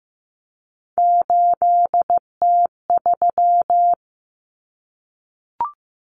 Morse code, heard as a keyed tone, about 15 wpm, 700 Hz. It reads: 8T3
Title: Common Cut Numbers - 3-Digit 15wpm